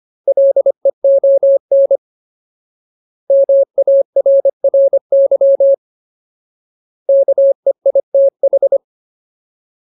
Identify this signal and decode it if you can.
Morse code, heard as a keyed tone, about 25 wpm, 550 Hz. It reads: LEON MARRY KEITH
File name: Ultimate Word List / 25WPM Ultimate